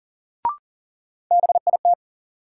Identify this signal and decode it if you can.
Morse code, heard as a keyed tone, about 38 wpm, 700 Hz. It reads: BIT
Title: Sets of 3 Words - New Word Drill - 40wpm